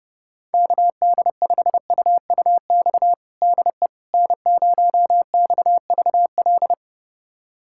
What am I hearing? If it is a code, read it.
Morse code, heard as a keyed tone, about 30 wpm, 700 Hz. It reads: KD5UUX DE N0XVL
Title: USA Callsign DE Callsign 30wpm